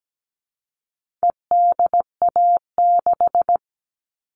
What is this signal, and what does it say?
Morse code, heard as a keyed tone, about 17 wpm, 700 Hz. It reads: EDA6
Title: All Cut Numbers - 4-Digit 17wpm